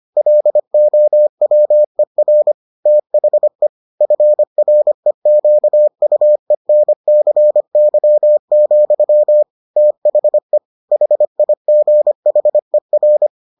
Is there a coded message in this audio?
Yes: Morse code from 0.2 to 13.3 s, heard as a keyed tone, about 25 wpm, 600 Hz. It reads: LOWER THE FREQUENCY, THE HIGHER